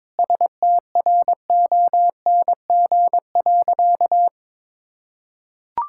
Morse code, heard as a keyed tone, about 22 wpm, 700 Hz. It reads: STRONG.
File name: practice / 22wpm / Sentences with 8-Word Phrases - 22wpm